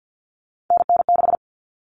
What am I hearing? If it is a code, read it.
Morse code, heard as a keyed tone, about 50 wpm, 700 Hz. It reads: NN6
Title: Common Cut Numbers - 3-Digit 50wpm